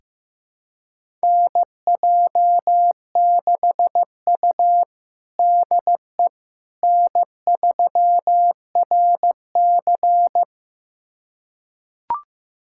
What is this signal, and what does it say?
Morse code, heard as a keyed tone, about 16 wpm, 700 Hz. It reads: NJ6U DE N3RC
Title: USA Callsign DE Callsign - 4 Characters - 15wpm